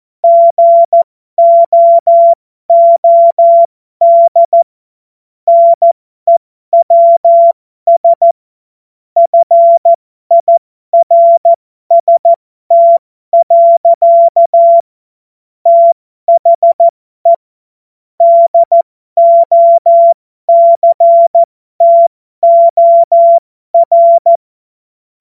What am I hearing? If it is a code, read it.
Morse code, heard as a keyed tone, about 14 wpm, spaced as for 12 wpm, 680 Hz. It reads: GOOD NEWS FIRST. THE DOCTOR
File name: Morsekurs / Witze